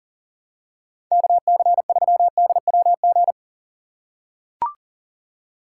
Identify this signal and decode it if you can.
Morse code, heard as a keyed tone, about 40 wpm, 700 Hz. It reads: KC3DWG
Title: USA Call Signs - 4x Character Spacing - 40wpm